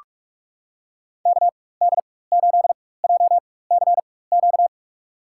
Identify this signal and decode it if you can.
Morse code, heard as a keyed tone, about 45 wpm, spaced as for 21 wpm, 700 Hz. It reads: KD8JCQ
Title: USA Call Signs - 6 Characters - 4x Character Spacing - 45wpm